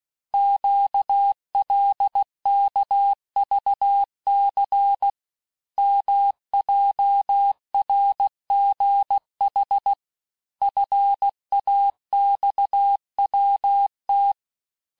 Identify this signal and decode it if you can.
Morse code, heard as a keyed tone, about 16 wpm, 800 Hz. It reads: QLKVC MJRGH FAXWT
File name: cours-du-lundi files / Cours-1586-texte-chiffre-vitesse-15mpm